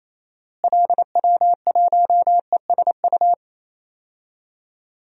Morse code, heard as a keyed tone, about 28 wpm, 700 Hz. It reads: LW1ESU
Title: DXCC-100 Call Signs - 6 Characters - Repeated 3x - 28wpm